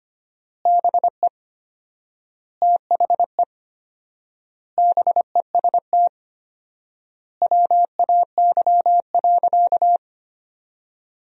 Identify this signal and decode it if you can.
Morse code, heard as a keyed tone, about 25 wpm, 700 Hz. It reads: BE THE BEST WAY.
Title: 4x Word-Spacing - Sentences from Top 400 Words - 25wpm